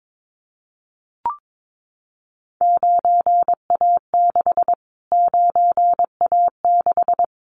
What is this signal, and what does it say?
Morse code, heard as a keyed tone, about 22 wpm, 700 Hz. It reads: 9A6 9A6
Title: Callsign Prefixes for Top 25 DXCC - Repeated 3x - 22wpm